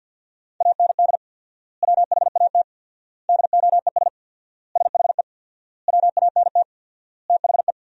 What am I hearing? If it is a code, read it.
Morse code, heard as a keyed tone, about 48 wpm, 700 Hz. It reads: AND WHAT DOES SHE WANT THE
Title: 4x Word-Spacing - Sentences from Top 400 Words - 50wpm